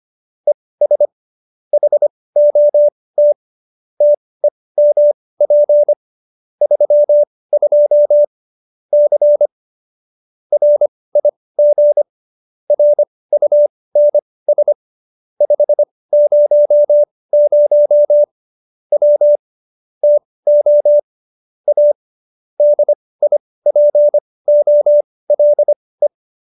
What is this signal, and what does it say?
Morse code, heard as a keyed tone, about 25 wpm, spaced as for 18 wpm, 590 Hz. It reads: ES HOT TEMP 32 C RIG RUNS 500 W TO A DIPOLE
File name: qso216b 18